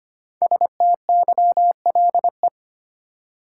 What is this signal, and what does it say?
Morse code, heard as a keyed tone, about 25 wpm, 700 Hz. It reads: STYLE
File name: 5 Letter Words - First 3 Letters Given - 25wpm